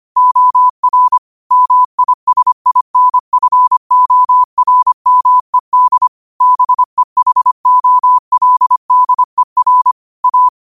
Morse code, heard as a keyed tone, about 25 wpm, 1000 Hz. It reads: OR MISINFORMED BEHOLDER A